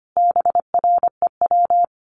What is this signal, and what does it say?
Morse code, heard as a keyed tone, about 25 wpm, 700 Hz. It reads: BREW
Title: Common QSO Phrases 25wpm